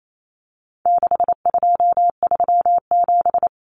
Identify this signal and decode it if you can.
Morse code, heard as a keyed tone, about 28 wpm, 700 Hz. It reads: 6237
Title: Numbers - 4-Digit 28wpm